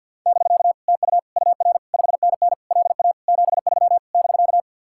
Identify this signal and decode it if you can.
Morse code, heard as a keyed tone, about 61 wpm, 700 Hz. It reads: <BK> TU UR 5NN LA 73 <BK>
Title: Speed-Racing Common QSO Phrases - 50wpm